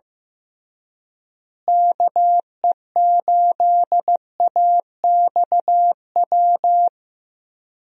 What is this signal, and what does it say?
Morse code, heard as a keyed tone, about 15 wpm, 700 Hz. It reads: KE8AXW
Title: USA Call Signs - 6 Characters - Repeated 3x - 15wpm